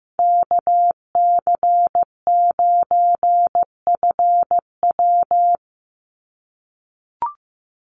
Morse code, heard as a keyed tone, about 15 wpm, 700 Hz. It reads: KC9FW
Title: USA Call Signs 15wpm